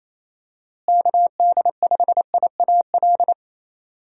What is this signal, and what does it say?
Morse code, heard as a keyed tone, about 28 wpm, 700 Hz. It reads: KD5IAL
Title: USA Call Signs - 6 Characters - Repeated 2x - 28wpm